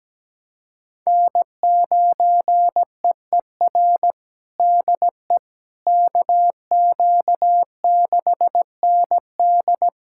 Morse code, heard as a keyed tone, about 17 wpm, 700 Hz. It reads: N9EER DE KQ6ND
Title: USA Callsign DE Callsign - 5 Characters - 17wpm